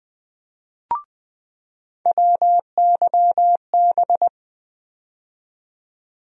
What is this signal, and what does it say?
Morse code, heard as a keyed tone, about 20 wpm, 700 Hz. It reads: WYB
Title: USA Call Sign Postfixes - 3 Letters 20wpm